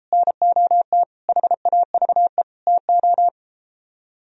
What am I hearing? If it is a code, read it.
Morse code, heard as a keyed tone, about 33 wpm, 700 Hz. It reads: NOT HAVE TO